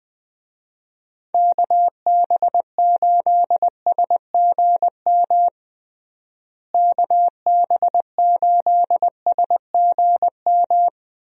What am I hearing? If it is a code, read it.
Morse code, heard as a keyed tone, about 20 wpm, 700 Hz. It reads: KB8SGM KB8SGM
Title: USA Call Signs - 6 Characters - Repeated 3x - 20wpm